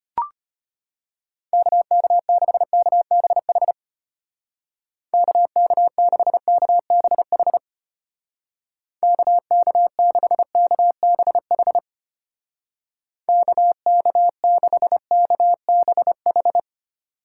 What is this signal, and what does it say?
Morse code, heard as a keyed tone, about 32 wpm, 700 Hz. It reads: KK6KBH KK6KBH KK6KBH KK6KBH